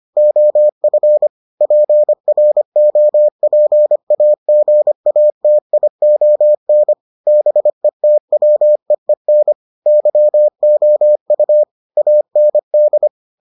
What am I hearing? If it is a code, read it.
Morse code, heard as a keyed tone, about 25 wpm, 600 Hz. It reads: OF PROPAGATION BETWEEN YOU AND